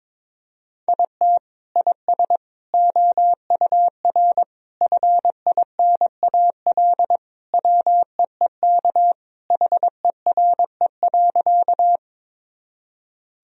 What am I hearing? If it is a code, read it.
Morse code, heard as a keyed tone, about 22 wpm, 700 Hz. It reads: IT IS OUR FINAL WEEK HERE.
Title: Sentences from Top 1000 Words - Review Part 2 - 22wpm